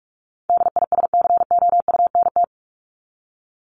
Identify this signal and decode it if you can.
Morse code, heard as a keyed tone, about 44 wpm, 700 Hz. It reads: DISCOUNT
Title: Top 1001-1250 Words 45wpm